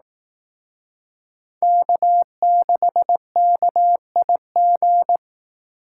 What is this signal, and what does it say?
Morse code, heard as a keyed tone, about 18 wpm, 700 Hz. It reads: K6KIG
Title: Speed Racing - USA 1x3 Call Signs 15wpm